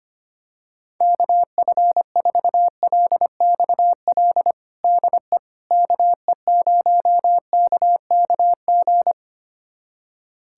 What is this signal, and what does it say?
Morse code, heard as a keyed tone, about 25 wpm, 700 Hz. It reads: KF4LXL DE KE0KKG
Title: USA Callsign DE Callsign - 6 Characters - 25wpm